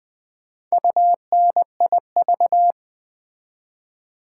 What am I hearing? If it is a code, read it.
Morse code, heard as a keyed tone, about 20 wpm, 700 Hz. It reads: UNIV
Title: Build A Word - 4 Letters At A Time - 10-Letter Words - 20wpm